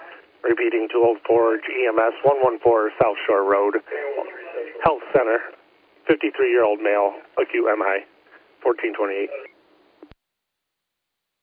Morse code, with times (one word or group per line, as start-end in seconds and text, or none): none